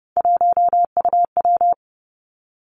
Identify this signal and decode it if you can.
Morse code, heard as a keyed tone, about 30 wpm, 700 Hz. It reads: 1UW
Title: Three Challenging Letter-Number Combinations 30wpm